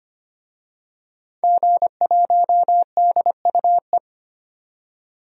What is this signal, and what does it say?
Morse code, heard as a keyed tone, about 25 wpm, 700 Hz. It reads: G1DUE
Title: DXCC-100 Call Signs - 5 Characters - 25wpm